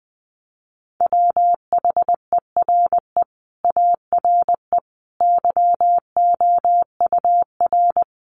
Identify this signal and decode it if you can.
Morse code, heard as a keyed tone, about 20 wpm, 700 Hz. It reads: WHERE ARE YOUR
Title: Sentences from Top 1200 Words 20wpm